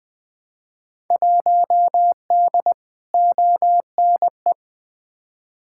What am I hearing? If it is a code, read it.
Morse code, heard as a keyed tone, about 20 wpm, 700 Hz. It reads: 1D ONE
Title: Field Day Exchanges - 20wpm